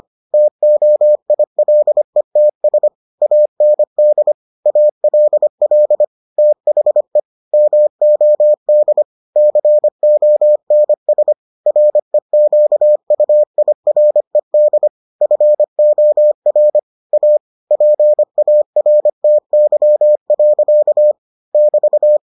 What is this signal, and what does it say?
Morse code, heard as a keyed tone, about 25 wpm, 600 Hz. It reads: TOILETS AND ALL THE MOD CONS REQUIRED FOR A PARTY. <BT>